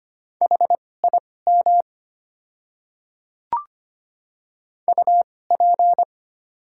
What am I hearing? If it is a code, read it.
Morse code, heard as a keyed tone, about 25 wpm, spaced as for 18 wpm, 690 Hz. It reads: HIM UP
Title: 2x Character Spacing - Top 100 Words - 25wpm